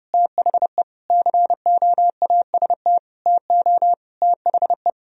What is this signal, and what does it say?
Morse code, heard as a keyed tone, about 29 wpm, 700 Hz. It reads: THE COAST TO THE
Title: Sentences from Top 1000 Words - Review Part 4 - 30wpm